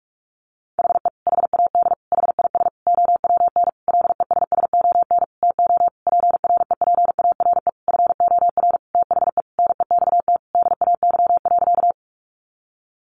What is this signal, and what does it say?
Morse code, heard as a keyed tone, about 44 wpm, 700 Hz. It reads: HE HAD HIS OWN LESSON TO PREPARE FOR THE NEXT DAY.
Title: Sentences from Top 1200 Words 45wpm